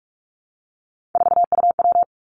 Morse code, heard as a keyed tone, about 46 wpm, 700 Hz. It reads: 4UW
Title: Three Challenging Letter-Number Combinations 45wpm